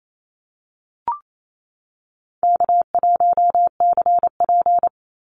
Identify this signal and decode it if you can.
Morse code, heard as a keyed tone, about 28 wpm, 700 Hz. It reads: K1CP
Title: Easy Call Signs - 28wpm